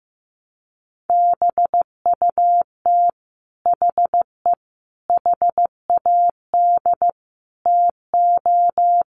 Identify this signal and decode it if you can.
Morse code, heard as a keyed tone, about 15 wpm, 700 Hz. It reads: BUT HE HAD TO
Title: Sentences from Top 100 Words 15wpm